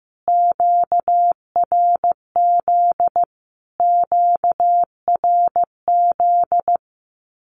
Morse code, heard as a keyed tone, about 15 wpm, 700 Hz. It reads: QRZ QRZ
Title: Q-Codes Repeated 3x 15wpm